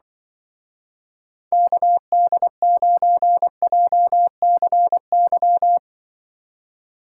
Morse code, heard as a keyed tone, about 24 wpm, 700 Hz. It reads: KD9JCY
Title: Speed Racing - USA 2x3 Call Signs 20wpm